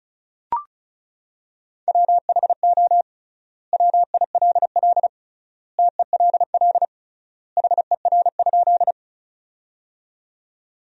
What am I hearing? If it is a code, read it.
Morse code, heard as a keyed tone, about 34 wpm, 700 Hz. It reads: WHO WILL TELL HER?
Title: 3x Word-Spacing - Sentences from Top 200 Words - 35wpm